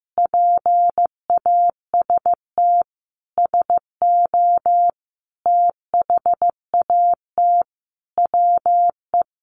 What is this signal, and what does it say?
Morse code, heard as a keyed tone, about 15 wpm, 700 Hz. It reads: PAST SO THAT WE
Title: A Sound Of Thunder 15wpm